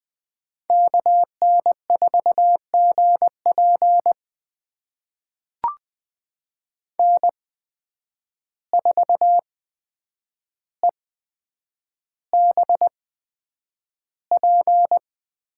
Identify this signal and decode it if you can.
Morse code, heard as a keyed tone, about 20 wpm, spaced as for 9 wpm, 700 Hz. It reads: KN4GP N4EBP
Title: USA Call Signs - 5 Characters - 8x Character Spacing - 20wpm